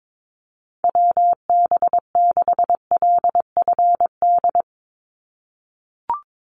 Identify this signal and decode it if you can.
Morse code, heard as a keyed tone, about 22 wpm, 700 Hz. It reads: WB6LFD
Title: USA Call Signs 22wpm